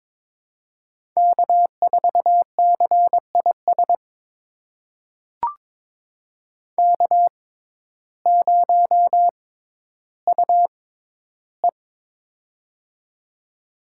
Morse code, heard as a keyed tone, about 22 wpm, spaced as for 12 wpm, 700 Hz. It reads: K4CIS K0UE
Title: USA Call Signs - 6x Character Spacing - 22wpm